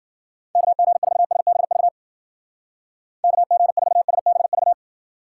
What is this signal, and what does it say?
Morse code, heard as a keyed tone, about 50 wpm, 700 Hz. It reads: KG4SBV KG4SBV
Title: Speed Racing - USA 2x3 Call Signs 40wpm